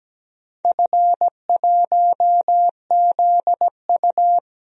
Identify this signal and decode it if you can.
Morse code, heard as a keyed tone, about 17 wpm, 700 Hz. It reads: F1ZU
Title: DXCC-100 Call Signs - 4 Characters - 17wpm